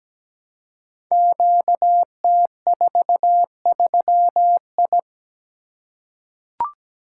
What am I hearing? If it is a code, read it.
Morse code, heard as a keyed tone, about 17 wpm, 700 Hz. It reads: QT43I
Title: Code Groups - 5 Letter:Numbers - 17wpm